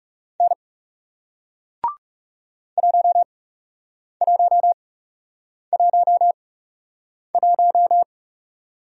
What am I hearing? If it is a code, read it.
Morse code, heard as a keyed tone, about 39 wpm, spaced as for 8 wpm, 700 Hz. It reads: N 1111